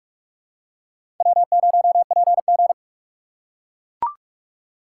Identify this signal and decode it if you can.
Morse code, heard as a keyed tone, about 43 wpm, 690 Hz. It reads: W0PG
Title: USA Call Signs - 4 Characters - 2x Character Spacing - 45wpm